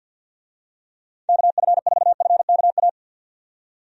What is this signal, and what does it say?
Morse code, heard as a keyed tone, about 48 wpm, 700 Hz. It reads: KF4FKA
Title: USA Call Signs - 6 Characters - 50wpm